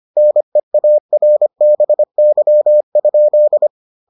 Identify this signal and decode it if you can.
Morse code, heard as a keyed tone, about 25 wpm, 600 Hz. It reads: NEARBY?